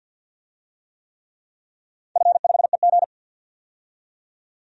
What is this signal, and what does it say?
Morse code, heard as a keyed tone, about 52 wpm, 700 Hz. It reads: F5EG
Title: DXCC-100 Call Signs - 4 Characters - 50wpm